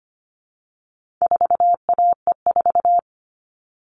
Morse code, heard as a keyed tone, about 25 wpm, 700 Hz. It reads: 4AE4
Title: All Cut Numbers - 4-Digit 25wpm